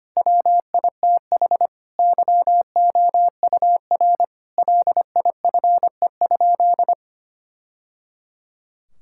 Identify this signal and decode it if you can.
Morse code, heard as a keyed tone, about 25 wpm, 700 Hz. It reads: WITH YOUR LIFE?
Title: Sentences from Top 1000 Words - Review Part 6 - 25wpm